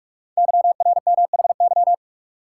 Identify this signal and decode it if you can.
Morse code, heard as a keyed tone, about 44 wpm, 700 Hz. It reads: YRMHY
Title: Code Groups - 5 Letters - 45wpm